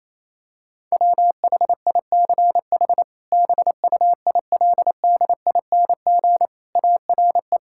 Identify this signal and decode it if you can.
Morse code, heard as a keyed tone, about 28 wpm, 700 Hz. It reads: WHICH BUILDING ARE
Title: Sentences from Top 1000 Words - Review Part 5 - 28wpm